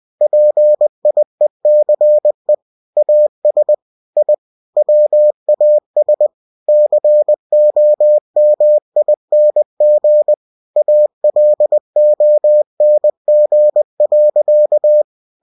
Morse code, heard as a keyed tone, about 20 wpm, 600 Hz. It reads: PIECE AS I WAS COMING ALONG.